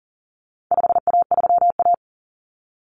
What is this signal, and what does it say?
Morse code, heard as a keyed tone, about 40 wpm, 700 Hz. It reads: 5A3A
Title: Common Cut Numbers - 4-Digit 40wpm